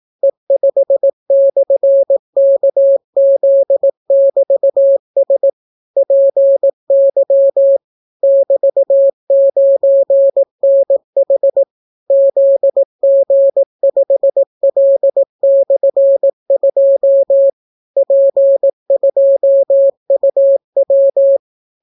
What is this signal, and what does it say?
Morse code, heard as a keyed tone, about 18 wpm, 550 Hz. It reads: E5/KZ<BT>S PY <BT>9NH ZG5L/2 P2UW